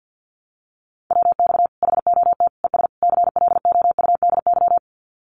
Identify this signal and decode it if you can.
Morse code, heard as a keyed tone, about 48 wpm, 700 Hz. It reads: WX HOT ES CLOUDY
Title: Common QSO Phrases 50wpm